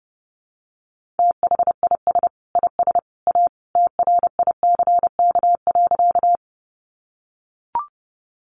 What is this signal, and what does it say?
Morse code, heard as a keyed tone, about 30 wpm, 700 Hz. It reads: THIS IS A TRICK.
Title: Sentences from Top 1500 Words 30wpm